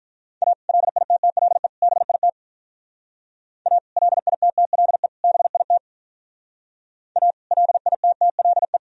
Morse code, heard as a keyed tone, about 44 wpm, 700 Hz. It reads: A LITTLE BIT A LITTLE BIT A LITTLE